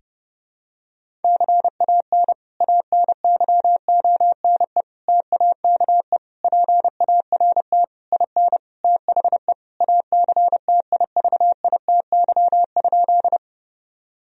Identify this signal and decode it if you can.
Morse code, heard as a keyed tone, about 32 wpm, 700 Hz. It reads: CAN ANYONE TAKE PART IN THE ACTIVITY?
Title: Sentences from Top 1000 Words - Review Part 4 - 30wpm